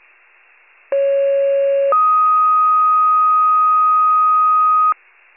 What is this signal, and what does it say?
Two-tone page, heard as a one-second tone, then a three-second tone, about 550 then 1250 Hz. It is heard from 0.9 to 4.9 s.
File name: wcalert